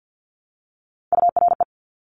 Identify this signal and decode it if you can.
Morse code, heard as a keyed tone, about 50 wpm, 700 Hz. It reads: URE
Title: Top Three Letter Combinations in Words 50wpm